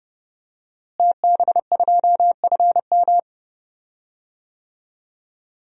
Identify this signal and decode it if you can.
Morse code, heard as a keyed tone, about 30 wpm, 700 Hz. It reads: TB2FM
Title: DXCC-100 Call Signs - 5 Characters - Repeated 2x - 30wpm